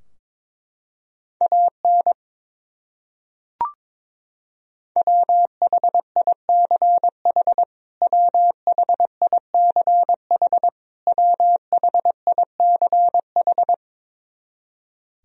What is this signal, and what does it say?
Morse code, heard as a keyed tone, about 23 wpm, 700 Hz. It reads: AN WHICH WHICH WHICH
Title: Top 100 Words Repeated 22wpm